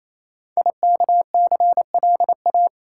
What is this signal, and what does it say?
Morse code, heard as a keyed tone, about 28 wpm, 700 Hz. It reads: IKCLA
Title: Code Groups - 5 Letter:Numbers - 28wpm